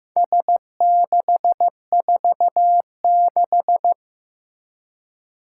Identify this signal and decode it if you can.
Morse code, heard as a keyed tone, about 15 wpm, 700 Hz. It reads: S646
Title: Four Challenging Letter-Number Combinations 15wpm